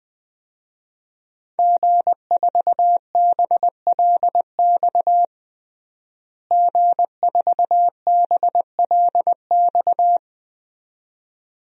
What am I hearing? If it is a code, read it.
Morse code, heard as a keyed tone, about 21 wpm, 700 Hz. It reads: G4BLX G4BLX
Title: DXCC-100 Call Signs - 5 Characters - Repeated 2x - 20wpm